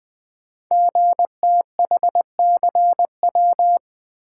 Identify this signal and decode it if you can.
Morse code, heard as a keyed tone, about 20 wpm, 700 Hz. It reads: GTHCW
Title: Code Groups - 5 Letters - 20wpm